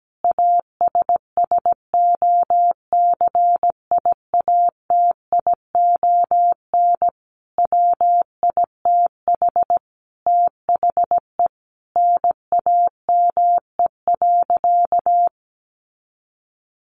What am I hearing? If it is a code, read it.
Morse code, heard as a keyed tone, about 17 wpm, 700 Hz. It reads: ASSOCIATION WITH THE NAME.
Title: Sentences from Top 1100 Words 17wpm